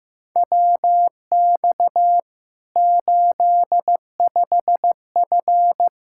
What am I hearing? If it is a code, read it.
Morse code, heard as a keyed tone, about 16 wpm, 700 Hz. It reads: WX 85F
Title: Practice QSO - Exchange 2 - Info - 15wpm